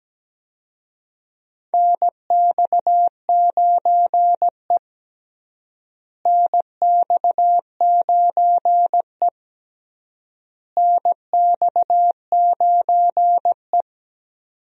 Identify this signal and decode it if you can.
Morse code, heard as a keyed tone, about 17 wpm, 700 Hz. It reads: NX9E NX9E NX9E
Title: USA Call Signs - 4 Characters - Repeated 3x - 17wpm